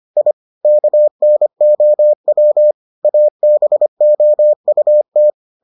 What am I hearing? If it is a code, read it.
Morse code, heard as a keyed tone, about 25 wpm, 600 Hz. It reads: I KNOW ABOUT